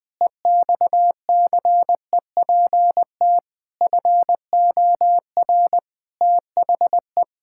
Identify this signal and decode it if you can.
Morse code, heard as a keyed tone, about 20 wpm, 700 Hz. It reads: EXCEPT FOR THE
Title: Pangram Challenge v2 - 20wpm